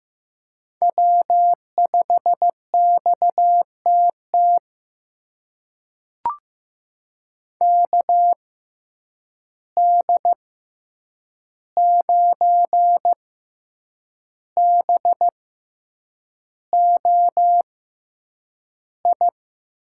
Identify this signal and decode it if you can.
Morse code, heard as a keyed tone, about 15 wpm, spaced as for 7 wpm, 700 Hz. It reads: W5XTT KD9BOI
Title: USA Call Signs - 6x Character Spacing - 15wpm